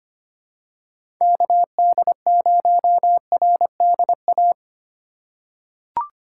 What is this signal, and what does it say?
Morse code, heard as a keyed tone, about 25 wpm, 700 Hz. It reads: KD0RDA
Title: USA Call Signs - 2x Character Spacing - 25wpm